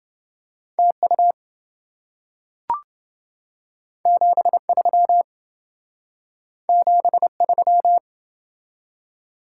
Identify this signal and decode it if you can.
Morse code, heard as a keyed tone, about 28 wpm, 700 Hz. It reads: TU 73 73